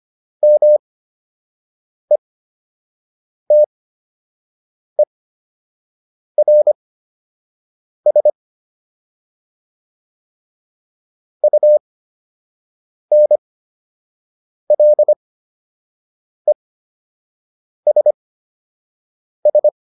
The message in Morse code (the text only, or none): METERS UNLESS